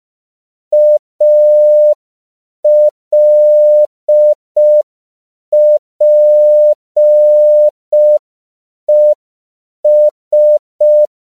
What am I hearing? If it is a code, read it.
Morse code, heard as a keyed tone, about 5 wpm, 600 Hz. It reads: ALPES
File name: CW / 5PPM